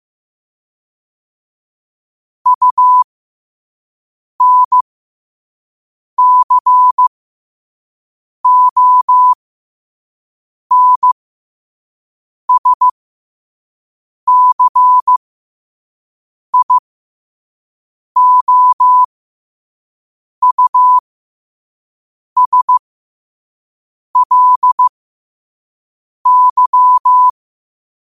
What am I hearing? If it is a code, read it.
Morse code, heard as a keyed tone, about 15 wpm, spaced as for 5 wpm, 1000 Hz. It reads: UNCONSCIOUSLY
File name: qotdfiles / 5wpm